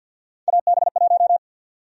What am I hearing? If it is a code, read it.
Morse code, heard as a keyed tone, about 52 wpm, 700 Hz. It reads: AB1
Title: USA Call Sign Prefixes - Repeated 2x - 50wpm